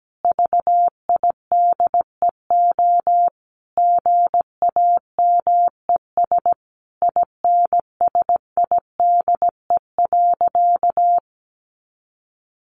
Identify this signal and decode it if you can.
Morse code, heard as a keyed tone, about 17 wpm, 700 Hz. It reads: VIDEO GAMES INSIDE.
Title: Easy Phrases 17wpm